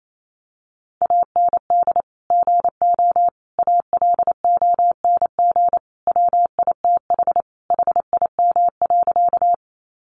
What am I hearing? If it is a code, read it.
Morse code, heard as a keyed tone, about 28 wpm, 700 Hz. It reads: AND GO ALONG WITH HIM.
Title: Sentences from Top 700 Words 28wpm